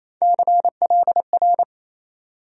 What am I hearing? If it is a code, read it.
Morse code, heard as a keyed tone, about 28 wpm, 700 Hz. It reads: CLR
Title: USA Call Sign Postfixes - 3 Letters 28wpm